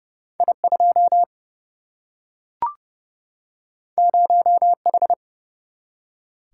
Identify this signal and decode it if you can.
Morse code, heard as a keyed tone, about 30 wpm, 700 Hz. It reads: I2 0H
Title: Single-Letter Single-Number 30wpm